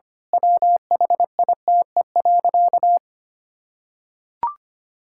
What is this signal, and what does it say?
Morse code, heard as a keyed tone, about 25 wpm, 700 Hz. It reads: WHITE.
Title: Pangram Challenge v2 - 25wpm